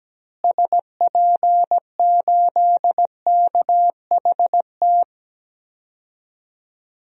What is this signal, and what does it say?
Morse code, heard as a keyed tone, about 17 wpm, 700 Hz. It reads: SP8KHT
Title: DXCC-100 Call Signs - 6 Characters - Repeated 2x - 17wpm